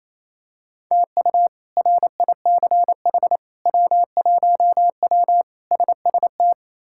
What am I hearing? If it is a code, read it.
Morse code, heard as a keyed tone, about 27 wpm, 700 Hz. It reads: TU RICH W1W SST
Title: K1USN SST QSO Practice - 28wpm